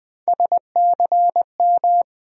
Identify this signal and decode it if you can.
Morse code, heard as a keyed tone, about 20 wpm, 700 Hz. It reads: SCM
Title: USA Call Sign Postfixes - 3 Letters 20wpm